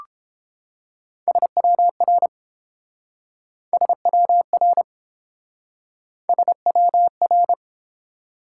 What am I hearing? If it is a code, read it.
Morse code, heard as a keyed tone, about 29 wpm, 700 Hz. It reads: SWR SWR SWR